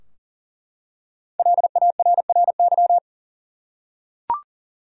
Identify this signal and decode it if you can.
Morse code, heard as a keyed tone, about 40 wpm, 700 Hz. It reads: LARRY
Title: USA Common Names - 40wpm